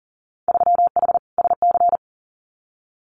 Morse code, heard as a keyed tone, about 41 wpm, 700 Hz. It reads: 3H SC
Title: Winter Field Day Exchanges 40wpm